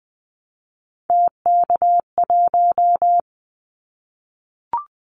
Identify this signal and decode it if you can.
Morse code, heard as a keyed tone, about 20 wpm, 700 Hz. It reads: TK1
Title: Callsign Prefixes for Top 100 DXCC - 20wpm